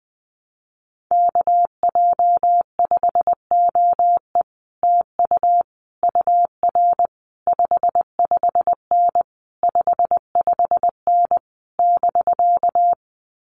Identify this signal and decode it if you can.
Morse code, heard as a keyed tone, about 19 wpm, 700 Hz. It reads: KJ5OE TU UR 55N 55N <BK>
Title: Common QSO Phrases Encore 20wpm